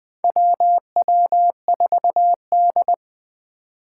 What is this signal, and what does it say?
Morse code, heard as a keyed tone, about 21 wpm, 700 Hz. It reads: WW4D
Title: 2023 SOTA Activator Call Signs - 20wpm